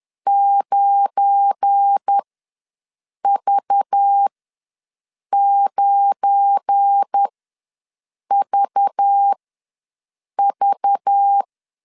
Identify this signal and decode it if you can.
Morse code, heard as a keyed tone, about 10 wpm, spaced as for 6 wpm, 800 Hz. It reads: 9V9VV